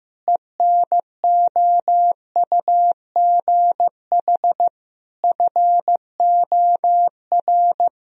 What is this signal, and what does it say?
Morse code, heard as a keyed tone, about 16 wpm, 700 Hz. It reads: ENOUGH FOR
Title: Sentences from Top 1000 Words - Review Part 1 - 15wpm